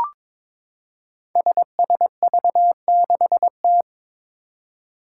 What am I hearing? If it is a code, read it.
Morse code, heard as a keyed tone, about 22 wpm, 700 Hz. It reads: SSV6T
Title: Five Challenging Letter-Number Combinations 22wpm